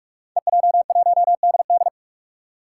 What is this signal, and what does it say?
Morse code, heard as a keyed tone, about 46 wpm, 700 Hz. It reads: EJ1DD
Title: Five Challenging Letter-Number Combinations 45wpm